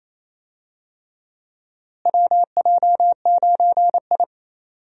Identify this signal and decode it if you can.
Morse code, heard as a keyed tone, about 28 wpm, 700 Hz. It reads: WJ9I
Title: USA Call Signs - 4 Characters - Repeated 3x - 28wpm